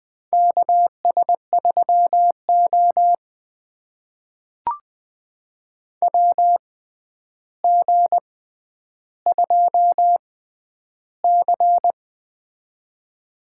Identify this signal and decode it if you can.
Morse code, heard as a keyed tone, about 20 wpm, spaced as for 10 wpm, 700 Hz. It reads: KS3O WG2C
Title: USA Call Signs - 4 Characters - 6x Character Spacing - 20wpm